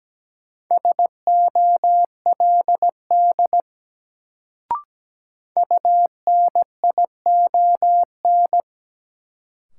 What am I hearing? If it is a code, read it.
Morse code, heard as a keyed tone, about 17 wpm, 700 Hz. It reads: SOLD UNION
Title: Sentences from Top 1400 Words - New Word Drill - 17wpm